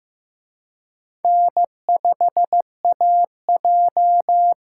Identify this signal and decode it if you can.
Morse code, heard as a keyed tone, about 18 wpm, 700 Hz. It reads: N5AJ
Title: Speed Racing - USA 1x2 Call Signs 15wpm